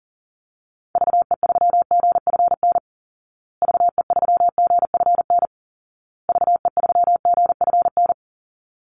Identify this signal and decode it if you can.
Morse code, heard as a keyed tone, about 40 wpm, 700 Hz. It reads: VE3GFN VE3GFN VE3GFN
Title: Call Signs Repeated 2x - 40wpm